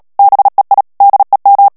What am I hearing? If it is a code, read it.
Morse code, heard as a keyed tone, about 37 wpm, 800 Hz. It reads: BEI DEM
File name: morse mp3 / ALLEMAND texte clair 2 vitesse 35 mots minute